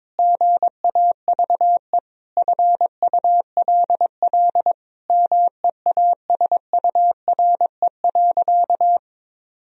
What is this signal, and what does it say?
Morse code, heard as a keyed tone, about 22 wpm, 700 Hz. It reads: GAVE FULL MEASURE.
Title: Sentences from Top 1000 Words - Review Part 3 - 22wpm